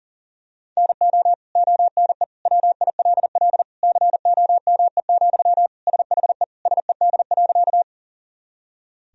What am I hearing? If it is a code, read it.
Morse code, heard as a keyed tone, about 40 wpm, 700 Hz. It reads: NO ONE WILL COME, SHE SED.
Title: Sentences from Top 100 Words and Abbreviations 40wpm